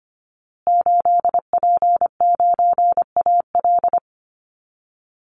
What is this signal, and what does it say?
Morse code, heard as a keyed tone, about 25 wpm, 700 Hz. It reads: 8P9AL
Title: Call Signs Repeated 3x - 25wpm